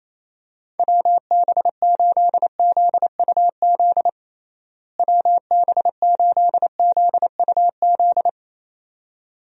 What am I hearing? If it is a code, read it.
Morse code, heard as a keyed tone, about 28 wpm, 700 Hz. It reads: WB8ZUZ WB8ZUZ
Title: USA Call Signs - 6 Characters - Repeated 2x - 28wpm